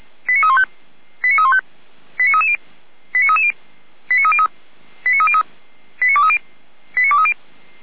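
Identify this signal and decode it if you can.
Five-tone selective call (ZVEI), heard as quick runs of five tones, tones of about 70 ms. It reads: 89315 89315 89330 89330 89303 89303 89230 89230
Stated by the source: ZVEI five-tone call 89320 received by SDR.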